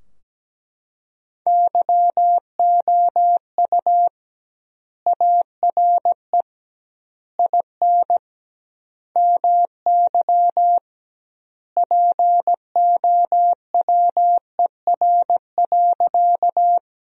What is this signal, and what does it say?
Morse code, heard as a keyed tone, about 17 wpm, 700 Hz. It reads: YOU ARE IN MY POWER.
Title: 2x Word-Spacing - Sentences from Top 500 Words - 17wpm